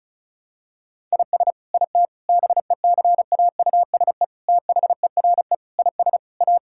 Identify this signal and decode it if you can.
Morse code, heard as a keyed tone, about 36 wpm, 700 Hz. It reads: IS IT BECAUSE THERE IS A
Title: Sentences from Top 1000 Words - Review Part 5 - 35wpm